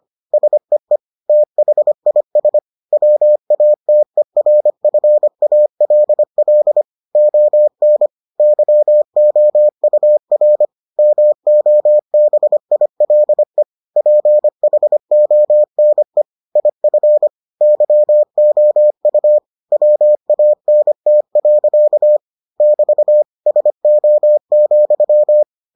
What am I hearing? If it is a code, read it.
Morse code, heard as a keyed tone, about 25 wpm, 600 Hz. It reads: SEE THIS WATERFALL ON YOUR MOBILE PHONE IF YOU WANT. <BT> SO,